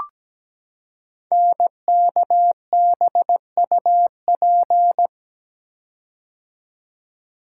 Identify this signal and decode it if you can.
Morse code, heard as a keyed tone, about 17 wpm, 700 Hz. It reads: NKBUP
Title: Code Groups - 5 Letters - 17wpm